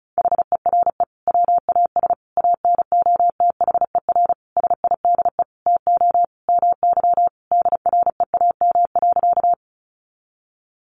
Morse code, heard as a keyed tone, about 34 wpm, 700 Hz. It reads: HERE WAS ANOTHER SIDE TO MY DREAM.
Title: Sentences from Top 1500 Words 35wpm